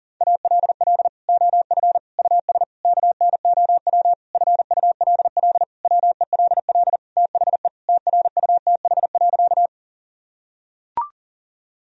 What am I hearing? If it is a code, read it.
Morse code, heard as a keyed tone, about 40 wpm, 700 Hz. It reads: ALL OF US KNOW FULL WELL THE TRUTH.
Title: Sentences from Top 1000 Words - Review Part 3 - 40wpm